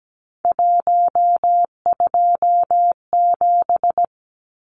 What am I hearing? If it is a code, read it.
Morse code, heard as a keyed tone, about 17 wpm, 700 Hz. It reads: 127
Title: Numbers - 3-Digit 17wpm